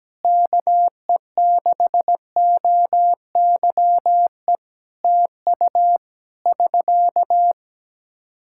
Morse code, heard as a keyed tone, about 17 wpm, 700 Hz. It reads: KE6OYE TU <SK>
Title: Practice QSO - Exchange 4 - Ending - 17wpm